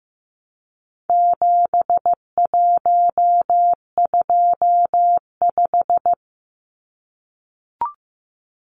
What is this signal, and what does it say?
Morse code, heard as a keyed tone, about 15 wpm, 700 Hz. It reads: 7125
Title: Numbers - 4-Digit 15wpm